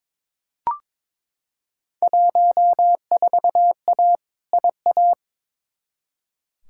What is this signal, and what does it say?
Morse code, heard as a keyed tone, about 22 wpm, 700 Hz. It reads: 14A IA
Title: Field Day Exchanges Encore - 22wpm